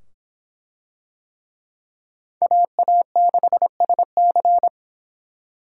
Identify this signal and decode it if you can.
Morse code, heard as a keyed tone, about 24 wpm, 700 Hz. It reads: AA6SC